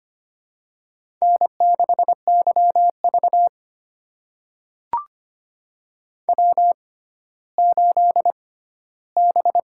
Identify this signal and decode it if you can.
Morse code, heard as a keyed tone, about 25 wpm, spaced as for 14 wpm, 700 Hz. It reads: N6YV W8B
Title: USA Call Signs - 4 Characters - 6x Character Spacing - 25wpm